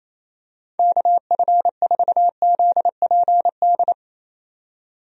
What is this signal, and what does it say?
Morse code, heard as a keyed tone, about 28 wpm, 700 Hz. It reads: KF4ZPD
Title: USA Call Signs - 6 Characters - 28wpm